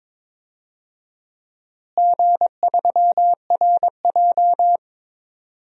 Morse code, heard as a keyed tone, about 22 wpm, 700 Hz. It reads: G3RJ
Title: DXCC-100 Call Signs - 4 Characters - Repeated 3x - 22wpm